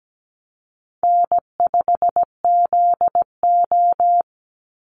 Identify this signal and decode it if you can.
Morse code, heard as a keyed tone, about 17 wpm, 700 Hz. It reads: N5ZO
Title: Call Signs Repeated 3x - 17wpm